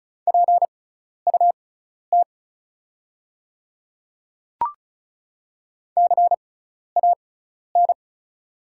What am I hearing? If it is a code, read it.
Morse code, heard as a keyed tone, about 34 wpm, spaced as for 12 wpm, 690 Hz. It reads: PUT CAN
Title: 6x Character Spacing - Top 200 Words - 35wpm